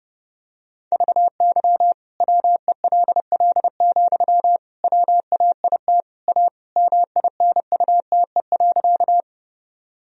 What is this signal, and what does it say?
Morse code, heard as a keyed tone, about 30 wpm, 700 Hz. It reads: VY WELL, WAIT A MINUTE.
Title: Sentences from Top 500 Words and Abbreviations 30wpm